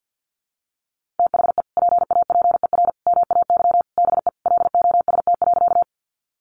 Morse code, heard as a keyed tone, about 50 wpm, 700 Hz. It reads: THE PAPER MAY BE LOST.
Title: Sentences from Top 1400 Words 50wpm